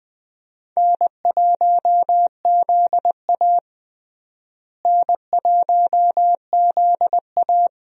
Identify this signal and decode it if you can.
Morse code, heard as a keyed tone, about 20 wpm, 700 Hz. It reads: N1ZA N1ZA
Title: USA Call Signs - 4 Characters - Repeated 2x - 20wpm